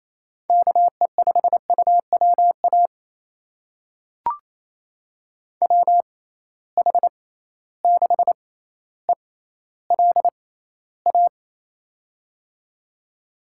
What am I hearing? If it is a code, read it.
Morse code, heard as a keyed tone, about 28 wpm, spaced as for 14 wpm, 700 Hz. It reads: KE5UWA WH6ELA
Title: USA Call Signs - 6 Characters - 6x Character Spacing - 28wpm